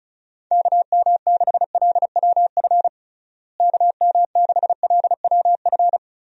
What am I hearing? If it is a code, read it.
Morse code, heard as a keyed tone, about 36 wpm, 700 Hz. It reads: KM6LWF KM6LWF
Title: USA Call Signs - 6 Characters - Repeated 2x - 35wpm